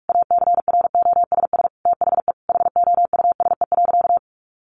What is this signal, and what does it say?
Morse code, heard as a keyed tone, about 44 wpm, 700 Hz. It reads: ACROSS THE HOUSE.
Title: Sentences from Top 700 Words 45wpm